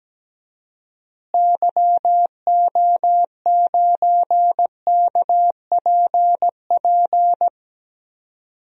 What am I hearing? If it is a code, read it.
Morse code, heard as a keyed tone, about 17 wpm, 700 Hz. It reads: YO9KPP
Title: DXCC-100 Call Signs - 6 Characters - Repeated 2x - 17wpm